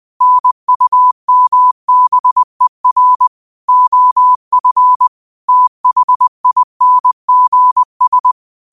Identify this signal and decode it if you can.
Morse code, heard as a keyed tone, about 20 wpm, 1000 Hz. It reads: NUMBER OF THINGS
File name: qotdfiles / 20wpm